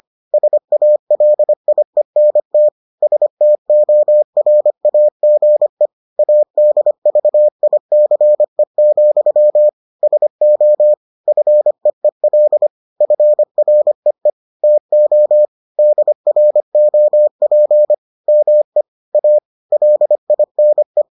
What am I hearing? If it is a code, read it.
Morse code, heard as a keyed tone, about 25 wpm, 600 Hz. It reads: SALIENT STORAGE ADVICE, SO FEEL FREE TO DROP ME A LINE